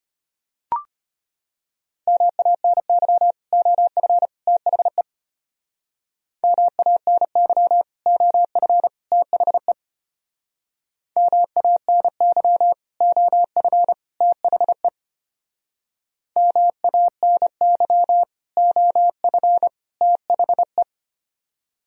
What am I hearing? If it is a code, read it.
Morse code, heard as a keyed tone, about 32 wpm, 700 Hz. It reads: MANY OF THE MANY OF THE MANY OF THE MANY OF THE